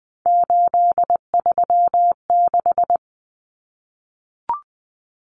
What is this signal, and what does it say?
Morse code, heard as a keyed tone, about 20 wpm, 700 Hz. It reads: 836
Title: Numbers - 3-Digit 20wpm